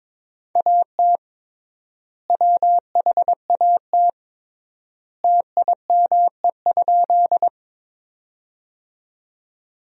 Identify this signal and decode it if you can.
Morse code, heard as a keyed tone, about 22 wpm, 700 Hz. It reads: AT WHAT TIME?
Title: 3x Word-Spacing - Sentences from Top 100 Words - 22wpm